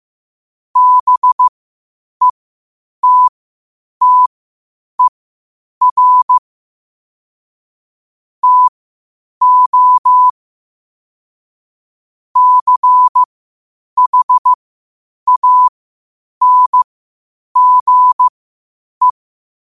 Morse code, heard as a keyed tone, about 15 wpm, spaced as for 8 wpm, 1000 Hz. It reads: BETTER TO CHANGE
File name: qotdfiles / 8wpm